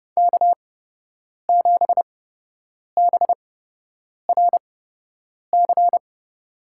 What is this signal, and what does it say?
Morse code, heard as a keyed tone, about 31 wpm, spaced as for 8 wpm, 700 Hz. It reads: K7BRC
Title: USA Call Signs - 8x Character Spacing - 30wpm